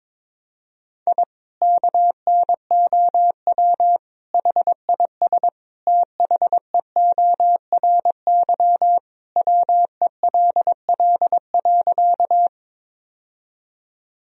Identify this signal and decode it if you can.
Morse code, heard as a keyed tone, about 22 wpm, 700 Hz. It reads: I KNOW HIS THEORY WELL.
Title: Sentences from Top 1000 Words - Review Part 3 - 22wpm